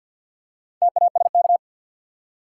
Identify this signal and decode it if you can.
Morse code, heard as a keyed tone, about 53 wpm, 700 Hz. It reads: TASK
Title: Four Letter Words - 50wpm